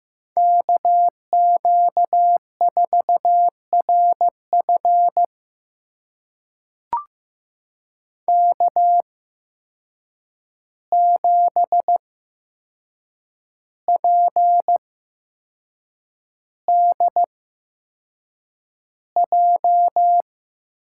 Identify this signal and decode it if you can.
Morse code, heard as a keyed tone, about 15 wpm, spaced as for 6 wpm, 700 Hz. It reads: KQ4RF K7PDJ
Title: USA Call Signs - 5 Characters - 8x Character Spacing - 15wpm